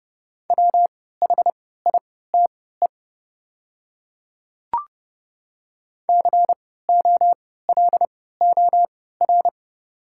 Morse code, heard as a keyed tone, about 30 wpm, spaced as for 17 wpm, 700 Hz. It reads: WHITE COLOR
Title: 3x Character Spacing - Top 500 Words - 30wpm